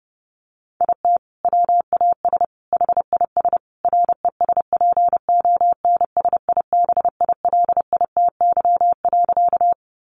Morse code, heard as a keyed tone, about 30 wpm, 700 Hz. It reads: IT WAS HIS RESPONSIBILITY.
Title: Sentences from Top 1200 Words 30wpm